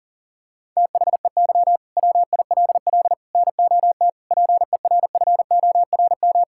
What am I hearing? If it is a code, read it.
Morse code, heard as a keyed tone, about 40 wpm, 700 Hz. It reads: THEY WILL NOT PERFORM
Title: Sentences from Top 1000 Words - Review Part 5 - 40wpm